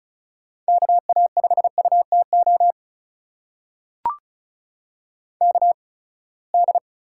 Morse code, heard as a keyed tone, about 36 wpm, spaced as for 24 wpm, 700 Hz. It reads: KA5UTO KD
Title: USA Call Signs - 6 Characters - 8x Character Spacing - 35wpm